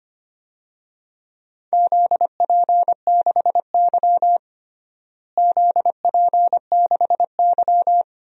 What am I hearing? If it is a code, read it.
Morse code, heard as a keyed tone, about 25 wpm, 700 Hz. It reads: ZP6Y ZP6Y
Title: DXCC-100 Call Signs - 4 Characters - Repeated 2x - 25wpm